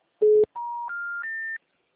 Special information tone.